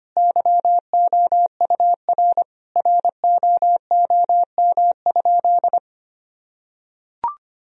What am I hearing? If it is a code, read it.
Morse code, heard as a keyed tone, about 25 wpm, 700 Hz. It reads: YOUR ROOM?
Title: Sentences from Top 1000 Words - Review Part 4 - 25wpm